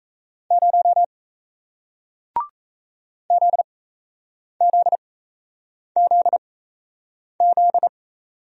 Morse code, heard as a keyed tone, about 38 wpm, spaced as for 8 wpm, 700 Hz. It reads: 0 ZZZZ